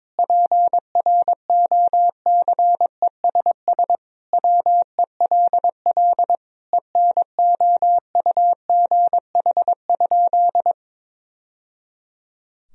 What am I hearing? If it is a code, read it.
Morse code, heard as a keyed tone, about 23 wpm, 700 Hz. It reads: PROCESS WELL ENOUGH?
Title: Sentences from Top 1000 Words - Review Part 3 - 22wpm